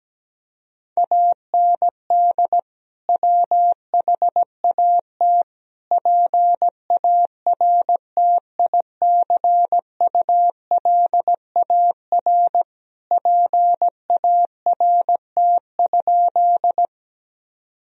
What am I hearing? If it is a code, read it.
Morse code, heard as a keyed tone, about 17 wpm, 700 Hz. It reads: AND WHAT PARTICULAR PART?
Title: Sentences from Top 1000 Words - Review Part 4 - 17wpm